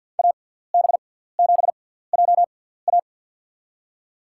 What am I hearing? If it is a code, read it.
Morse code, heard as a keyed tone, about 53 wpm, spaced as for 17 wpm, 700 Hz. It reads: AB7JA
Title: USA Call Signs - 5 Characters - 6x Character Spacing - 50wpm